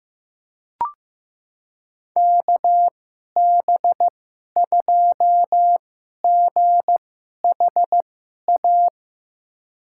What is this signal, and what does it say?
Morse code, heard as a keyed tone, about 15 wpm, spaced as for 11 wpm, 700 Hz. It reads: KB2GHA
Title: USA Call Signs - 6 Characters - 2x Character Spacing - 15wpm